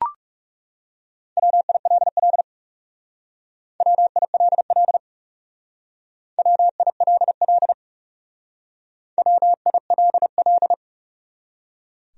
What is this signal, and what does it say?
Morse code, heard as a keyed tone, about 36 wpm, 700 Hz. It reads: WILL WILL WILL WILL
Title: Speed-Racing Top 100 Words 30wpm